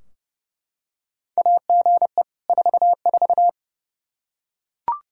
Morse code, heard as a keyed tone, about 29 wpm, 700 Hz. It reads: AGE 44
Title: Practice QSO - Exchange 3 - More Info - 30wpm